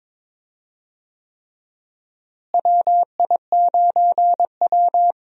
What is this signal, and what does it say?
Morse code, heard as a keyed tone, about 22 wpm, 700 Hz. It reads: WI9W
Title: USA Call Signs - 4 Characters - Repeated 3x - 22wpm